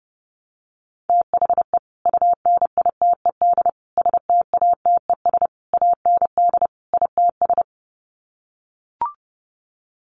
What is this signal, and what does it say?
Morse code, heard as a keyed tone, about 30 wpm, 700 Hz. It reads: THE UNITED STATES AND ITS
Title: Sets of 5 Words Encore 30wpm